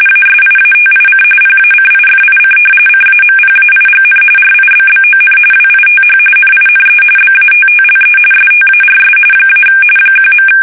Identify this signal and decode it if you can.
RTTY, about 45.45 baud. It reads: ASIATIC PACIFIC CAMPAIGN RIBBON WITH NINE BATTLE STARS - 1.
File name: fsk850 / battleshipnj